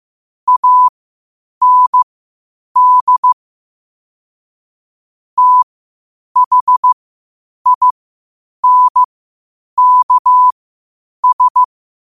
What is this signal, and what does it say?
Morse code, heard as a keyed tone, about 15 wpm, spaced as for 8 wpm, 1000 Hz. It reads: AND THINKS
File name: qotdfiles / 8wpm